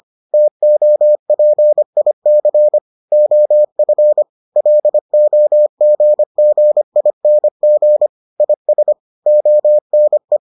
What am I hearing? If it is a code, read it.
Morse code, heard as a keyed tone, about 25 wpm, 600 Hz. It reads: TOPIC OF LOGGING IS ONE